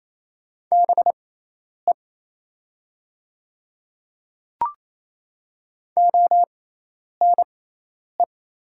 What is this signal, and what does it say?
Morse code, heard as a keyed tone, about 27 wpm, spaced as for 10 wpm, 700 Hz. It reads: BE ONE
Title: 6x Character Spacing - Top 100 Words - 28wpm